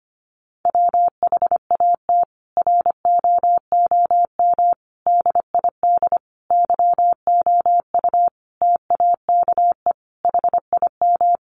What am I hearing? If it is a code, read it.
Morse code, heard as a keyed tone, about 25 wpm, 700 Hz. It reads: WHAT ROOM DID YOU TAKE HIM
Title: Sentences from Top 600 Words 25wpm